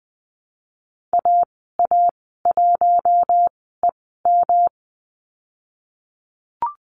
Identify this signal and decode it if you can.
Morse code, heard as a keyed tone, about 20 wpm, spaced as for 14 wpm, 700 Hz. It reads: AA1EM
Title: 128 Lesson 21g - Number 6 - Call Signs 20wpm